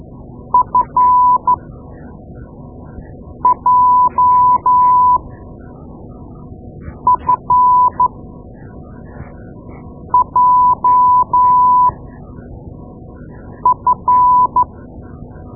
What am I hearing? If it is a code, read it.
Morse code, heard as a keyed tone, about 12 wpm, spaced as for 4 wpm, 1000 Hz. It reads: FJFJF
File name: Morse Lessons / Morse Lessons / Tape-2A